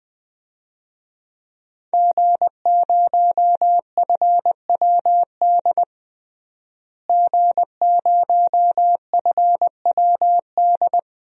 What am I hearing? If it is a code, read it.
Morse code, heard as a keyed tone, about 20 wpm, 700 Hz. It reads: G0FWD G0FWD
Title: DXCC-100 Call Signs - 5 Characters - Repeated 2x - 20wpm